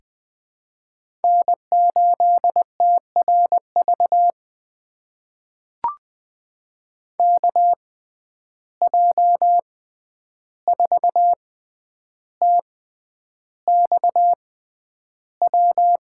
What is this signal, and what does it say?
Morse code, heard as a keyed tone, about 20 wpm, spaced as for 10 wpm, 700 Hz. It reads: N8TRV KJ4TXW
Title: USA Call Signs - 6x Character Spacing - 20wpm